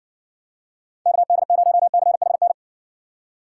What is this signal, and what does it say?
Morse code, heard as a keyed tone, about 60 wpm, 700 Hz. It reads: KD0CHN